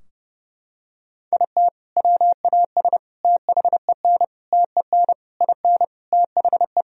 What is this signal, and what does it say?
Morse code, heard as a keyed tone, about 32 wpm, 700 Hz. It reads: IT WAS THEN TEN IN THE
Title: Sentences from Top 1000 Words - Review Part 2 - 30wpm